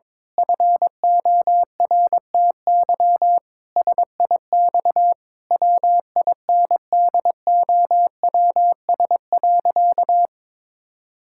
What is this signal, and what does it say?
Morse code, heard as a keyed tone, about 22 wpm, 700 Hz. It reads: FORTY SIX WINDOWS.
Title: Pangram Challenge v2 - 22wpm